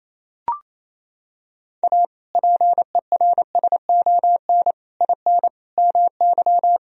Morse code, heard as a keyed tone, about 28 wpm, 700 Hz. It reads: A PERSON IN MY
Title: Sentences from Top 1000 Words - Review Part 4 - 28wpm